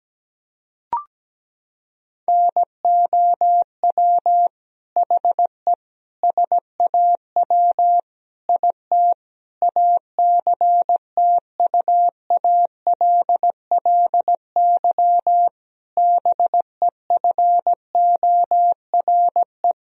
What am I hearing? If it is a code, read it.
Morse code, heard as a keyed tone, about 17 wpm, 700 Hz. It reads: NOW HE SAW IT ACTUALLY BEFORE
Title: Sentences from Top 1000 Words - Review Part 4 - 17wpm